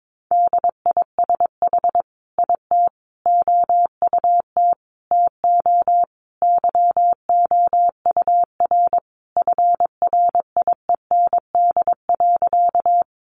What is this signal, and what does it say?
Morse code, heard as a keyed tone, about 22 wpm, 700 Hz. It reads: DISH IT OUT TO YOUR FRIEND.
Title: Sentences from Top 1300 Words 22wpm